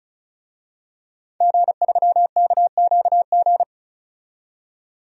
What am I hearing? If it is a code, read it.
Morse code, heard as a keyed tone, about 36 wpm, 700 Hz. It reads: G3KQG
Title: DXCC-100 Call Signs - 5 Characters - 35wpm